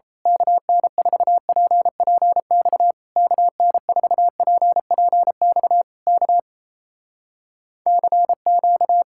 Morse code, heard as a keyed tone, about 26 wpm, 700 Hz. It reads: KN4PPX KN4PPX K CQ
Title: Speed-Racing Common QSO Phrases Encore - 28wpm